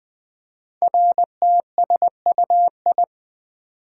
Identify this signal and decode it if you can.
Morse code, heard as a keyed tone, about 20 wpm, 700 Hz. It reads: RTSUI
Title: Code Groups - 5 Letters - 20wpm